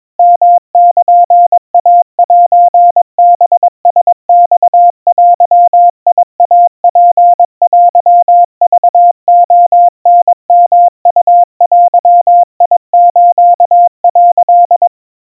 Morse code (Japanese Wabun code), heard as a keyed tone, about 22 wpm, 700 Hz. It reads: ヨルイセハラマデイツテクレタヨウデス」
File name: 80 cpm 700 Hz Crispy